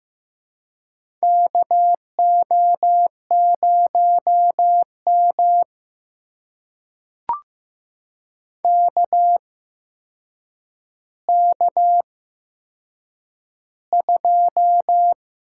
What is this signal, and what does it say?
Morse code, heard as a keyed tone, about 15 wpm, spaced as for 6 wpm, 700 Hz. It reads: KO0MK K2
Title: USA Call Signs - 8x Character Spacing - 15wpm